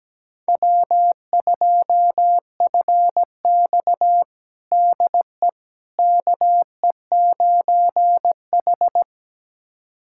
Morse code, heard as a keyed tone, about 17 wpm, 700 Hz. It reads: W2FX DE KE9H
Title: USA Callsign DE Callsign - 4 Characters - 17wpm